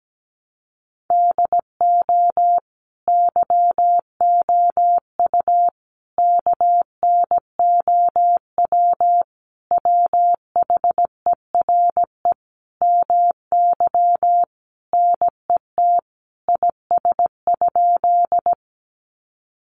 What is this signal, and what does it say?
Morse code, heard as a keyed tone, about 17 wpm, 700 Hz. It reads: DO YOU KNOW WHERE MY NET IS?
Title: Sentences from Top 1400 Words 17wpm